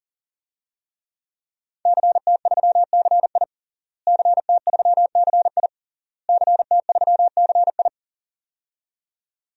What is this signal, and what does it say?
Morse code, heard as a keyed tone, about 37 wpm, 700 Hz. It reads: CT3CI CT3CI CT3CI
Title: DXCC-100 Call Signs - 5 Characters - Repeated 3x - 40wpm